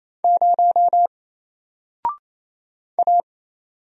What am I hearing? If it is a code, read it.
Morse code, heard as a keyed tone, about 27 wpm, 700 Hz. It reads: 0 A
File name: Single-Letter-Number-28wpm